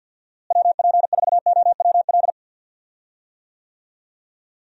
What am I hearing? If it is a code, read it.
Morse code, heard as a keyed tone, about 51 wpm, 700 Hz. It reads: WP4OWL
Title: USA Call Signs - 6 Characters - Repeated 2x - 50wpm